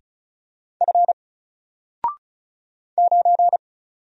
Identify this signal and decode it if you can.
Morse code, heard as a keyed tone, about 35 wpm, 700 Hz. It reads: F 9
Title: Single Character (Letter:Number:Punctuation:Prosign) 35wpm